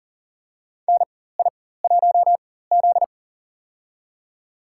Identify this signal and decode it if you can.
Morse code, heard as a keyed tone, about 40 wpm, spaced as for 19 wpm, 700 Hz. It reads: NI1Z
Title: USA Call Signs - 4x Character Spacing - 40wpm